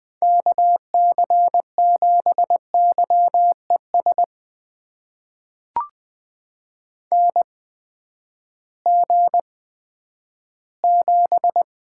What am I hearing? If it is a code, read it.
Morse code, heard as a keyed tone, about 20 wpm, spaced as for 11 wpm, 700 Hz. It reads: KC7YES NG7
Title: USA Call Signs - 8x Character Spacing - 20wpm